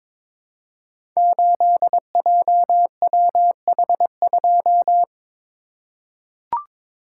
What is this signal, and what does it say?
Morse code, heard as a keyed tone, about 21 wpm, 700 Hz. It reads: 8JWH2
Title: Code Groups - 5 Letter:Numbers - 22wpm